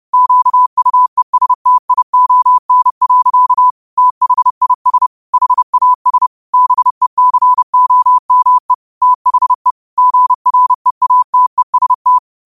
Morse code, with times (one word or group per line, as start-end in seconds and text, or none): 0.1-3.7 QUESTION.
4.0-5.1 THIS
5.3-6.3 HAS
6.5-8.7 BECOME
9.0-9.7 THE
10.0-12.2 GREATEST